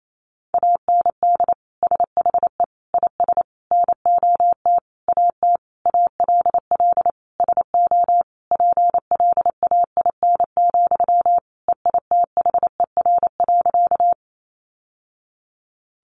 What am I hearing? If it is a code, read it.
Morse code, heard as a keyed tone, about 28 wpm, 700 Hz. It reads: AND SHE IS NOT AT ALL SO PLAIN, EITHER.
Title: Sentences from Top 600 Words 28wpm